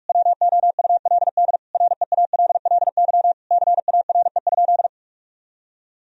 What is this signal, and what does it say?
Morse code, heard as a keyed tone, about 42 wpm, 700 Hz. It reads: WOULD REALLY CARE?
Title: Sentences from Top 1000 Words - Review Part 3 - 45wpm